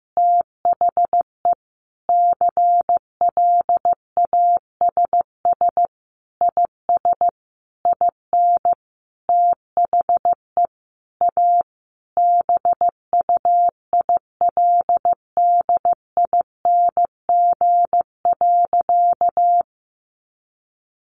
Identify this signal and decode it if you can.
Morse code, heard as a keyed tone, about 14 wpm, 700 Hz. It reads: THE CLASS IS IN THE A BUILDING.
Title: Additional Easy Phrases 15wpm